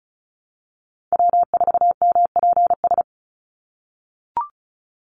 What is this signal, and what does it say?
Morse code, heard as a keyed tone, about 34 wpm, 700 Hz. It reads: W4MPS
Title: NA SOTA Chaser Callsigns 35wpm